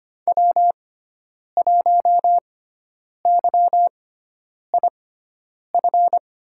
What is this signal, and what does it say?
Morse code, heard as a keyed tone, about 25 wpm, spaced as for 9 wpm, 700 Hz. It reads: W1YIF
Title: USA Call Signs - 6x Character Spacing - 25wpm